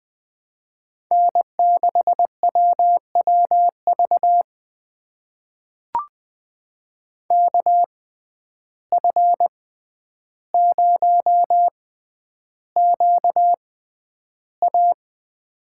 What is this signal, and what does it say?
Morse code, heard as a keyed tone, about 20 wpm, spaced as for 10 wpm, 700 Hz. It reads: N6WWV KF0QA
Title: USA Call Signs - 5 Characters - 6x Character Spacing - 20wpm